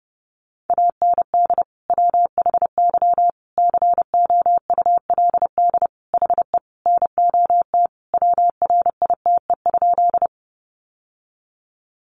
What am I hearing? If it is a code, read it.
Morse code, heard as a keyed tone, about 28 wpm, 700 Hz. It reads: AND WHY COULD HE NOT WRITE?
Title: Sentences from Top 600 Words 30wpm